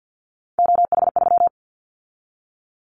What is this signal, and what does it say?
Morse code, heard as a keyed tone, about 50 wpm, 700 Hz. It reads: OH3
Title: Callsign Prefixes for Top 25 DXCC - 50wpm